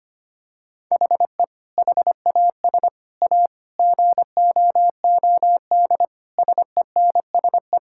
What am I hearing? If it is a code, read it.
Morse code, heard as a keyed tone, about 25 wpm, 700 Hz. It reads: HE HAS A GOOD SENSE
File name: practice / 25wpm / Sentences with 2-Word Phrases - 25wpm